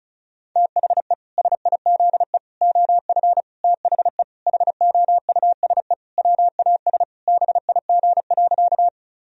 Morse code, heard as a keyed tone, about 34 wpm, 700 Hz. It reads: THE SIZE OF THE HOUSE WAS BIG.
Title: Sentences from Top 1000 Words - Review Part 2 - 35wpm